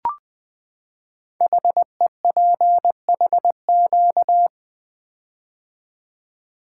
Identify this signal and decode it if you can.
Morse code, heard as a keyed tone, about 20 wpm, 700 Hz. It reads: HEPHQ
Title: Code Groups - 5 Letters - 20wpm